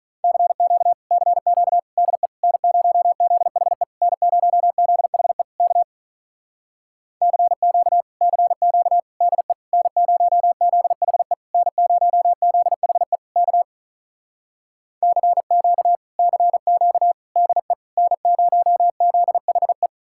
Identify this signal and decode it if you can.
Morse code, heard as a keyed tone, about 48 wpm, 700 Hz. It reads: CQ CQ DE N0ZHE N0ZHE K CQ CQ DE N0ZHE N0ZHE K CQ CQ DE N0ZHE